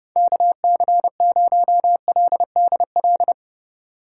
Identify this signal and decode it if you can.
Morse code, heard as a keyed tone, about 36 wpm, 700 Hz. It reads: KC0LDL
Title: Speed Racing - USA Call Signs 30wpm